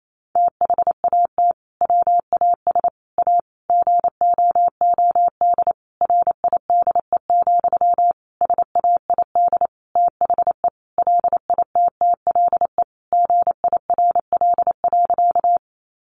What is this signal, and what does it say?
Morse code, heard as a keyed tone, about 28 wpm, 700 Hz. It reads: THAT WAS A GOOD RIDE, SAID THE LITTLE GIRL.
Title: Sentences from Top 1200 Words 28wpm